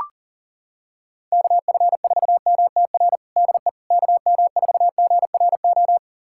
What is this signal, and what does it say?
Morse code, heard as a keyed tone, about 40 wpm, 700 Hz. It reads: KF4MTR DE KM4GRO
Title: USA Callsign DE Callsign 40wpm